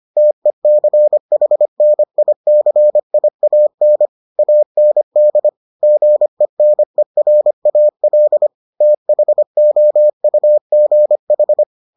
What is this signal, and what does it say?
Morse code, heard as a keyed tone, about 25 wpm, 600 Hz. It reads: TECHNICIAN AND GENERAL THOUGH